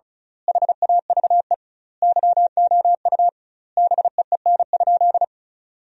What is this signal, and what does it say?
Morse code, heard as a keyed tone, about 36 wpm, 700 Hz. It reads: HAVE YOU BEEN?
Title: 2x Word-Spacing - Sentences from Top 200 Words - 35wpm